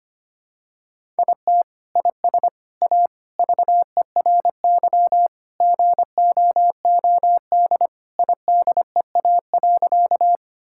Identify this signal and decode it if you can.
Morse code, heard as a keyed tone, about 25 wpm, 700 Hz. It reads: IT IS A VERY GOOD IDEA.
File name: practice / 25wpm / Sentences with 3-Word Phrases - 25wpm